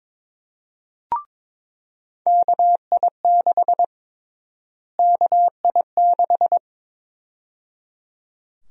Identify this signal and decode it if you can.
Morse code, heard as a keyed tone, about 21 wpm, 700 Hz. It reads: KI6 KI6
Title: USA Call Sign Prefixes - Repeated 2x - 22wpm